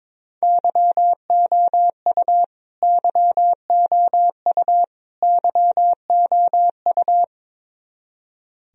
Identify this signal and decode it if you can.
Morse code, heard as a keyed tone, about 22 wpm, 700 Hz. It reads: YOU YOU YOU
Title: Top 100 Words Repeated 22wpm